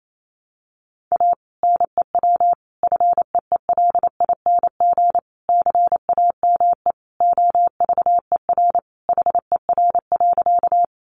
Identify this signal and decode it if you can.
Morse code, heard as a keyed tone, about 28 wpm, 700 Hz. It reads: A NEW FEELING CAME OVER HER.
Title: Sentences from Top 1200 Words 28wpm